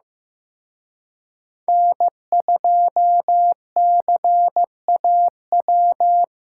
Morse code, heard as a keyed tone, about 18 wpm, 700 Hz. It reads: N2CAW
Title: Speed Racing - USA 1x3 Call Signs 15wpm